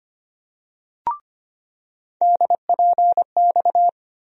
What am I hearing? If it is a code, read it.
Morse code, heard as a keyed tone, about 25 wpm, 700 Hz. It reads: DPX
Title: USA Call Sign Postfixes - 3 Letters 25wpm